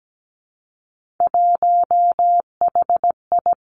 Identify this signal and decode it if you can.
Morse code, heard as a keyed tone, about 17 wpm, 700 Hz. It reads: 1HI
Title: Three Challenging Letter-Number Combinations 17wpm